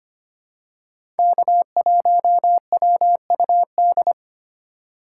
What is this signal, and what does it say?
Morse code, heard as a keyed tone, about 25 wpm, 700 Hz. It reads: K1WUD
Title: USA Call Signs - 5 Characters - 4x Character Spacing - 25wpm